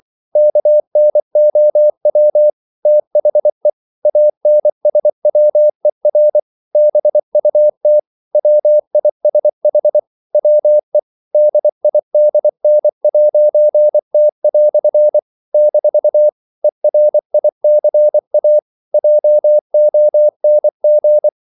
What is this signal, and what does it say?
Morse code, heard as a keyed tone, about 24 wpm, 600 Hz. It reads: KNOW THE ANSWER BUT WISH WE DIDN'T" - ERICA JONG
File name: CW / Practice / EnglishQuotations0224